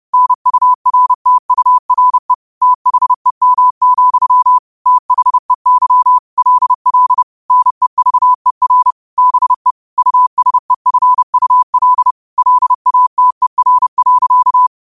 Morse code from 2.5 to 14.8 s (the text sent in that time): THEM, THEY LL NEVER BE USEFUL LATER.